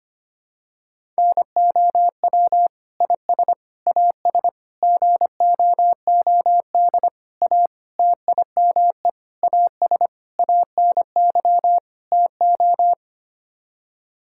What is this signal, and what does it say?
Morse code, heard as a keyed tone, about 25 wpm, 700 Hz. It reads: NOW IS AS GOOD A TIME AS ANY TO
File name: Sets of 9 Words - 25wpm